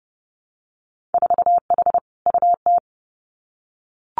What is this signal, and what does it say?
Morse code, heard as a keyed tone, about 30 wpm, 700 Hz. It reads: 4H UT
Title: Winter Field Day Exchanges Encore 30wpm